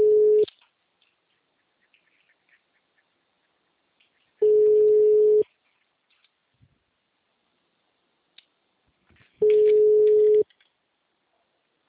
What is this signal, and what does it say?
Ringback tone (Europe, 425 Hz).
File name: out- -2018- - - .24487